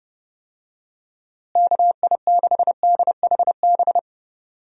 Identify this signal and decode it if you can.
Morse code, heard as a keyed tone, about 30 wpm, 700 Hz. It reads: KI6DHB
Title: USA Call Signs - 6 Characters - Repeated 3x - 30wpm